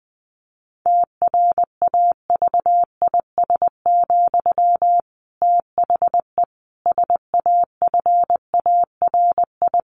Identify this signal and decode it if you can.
Morse code, heard as a keyed tone, about 20 wpm, 700 Hz. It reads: TRAVIS, THE SAFARI
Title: A Sound Of Thunder 20wpm